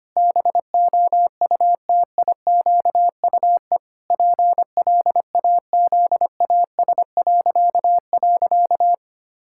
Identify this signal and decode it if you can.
Morse code, heard as a keyed tone, about 25 wpm, 700 Hz. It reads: BOUTIQUE PLAZAS..
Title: Pangram Challenge v2 - 25wpm